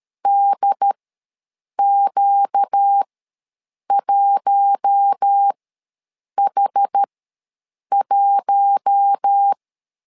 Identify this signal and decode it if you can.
Morse code, heard as a keyed tone, about 13 wpm, spaced as for 7 wpm, 800 Hz. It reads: DQ1H1
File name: LEZIONI / 4 Q1S ws 12wpm 1827